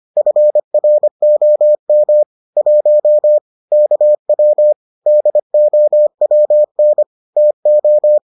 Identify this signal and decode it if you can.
Morse code, heard as a keyed tone, about 25 wpm, 600 Hz. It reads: FROM 1 KW DOWN TO